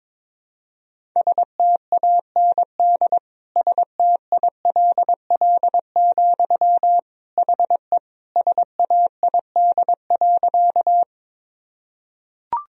Morse code, heard as a keyed tone, about 22 wpm, 700 Hz. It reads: STAND STILL, HE SAID.
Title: Sentences from Top 1000 Words - Review Part 1 - 22wpm